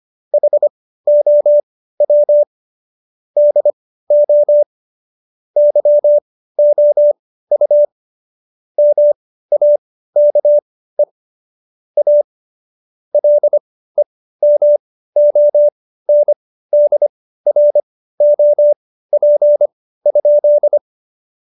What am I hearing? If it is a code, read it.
Morse code, heard as a keyed tone, about 25 wpm, spaced as for 15 wpm, 590 Hz. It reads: HOW DO YOU MAKE A LEMONDROP?
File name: Jokes-25-char-15-eff